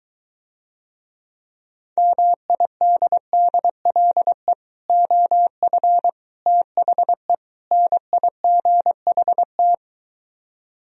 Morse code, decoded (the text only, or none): MIDDLE OF THE NIGHT